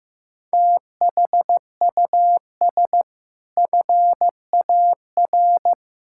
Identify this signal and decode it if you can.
Morse code, heard as a keyed tone, about 15 wpm, 700 Hz. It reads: THUS FAR
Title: Sentences from Top 1000 Words - Review Part 5 - 15wpm